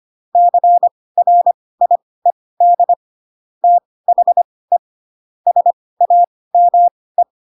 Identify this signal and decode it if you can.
Morse code, heard as a keyed tone, about 25 wpm, spaced as for 18 wpm, 700 Hz. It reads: CRIED THE SAME
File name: Pinocchio / Pinocchio ch2 12 25 ik1hge